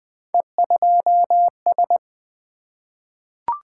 Morse code, heard as a keyed tone, about 20 wpm, 700 Hz. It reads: E2S
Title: Three Letter-Number Combinations - 20wpm